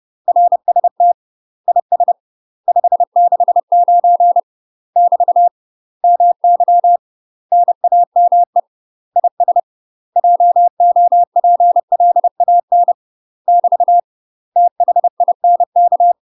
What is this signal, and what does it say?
Morse code, heard as a keyed tone, about 30 wpm, 700 Hz. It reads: RST IS 569 <BT> MY NAME IS JOPLAN <BT> THINK
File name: UK MASTER QSOs TRAINING / 30WPM / UK MASTER QSOs 0033